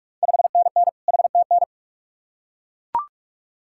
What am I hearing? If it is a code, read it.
Morse code, heard as a keyed tone, about 42 wpm, 700 Hz. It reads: 5NN HTN
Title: Ontario QSO Party Exchange 45wpm